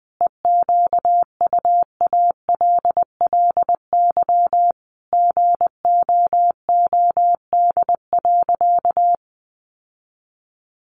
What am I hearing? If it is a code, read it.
Morse code, heard as a keyed tone, about 20 wpm, 700 Hz. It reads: EQUALLY GOOD.
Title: Sentences from Top 1400 Words 20wpm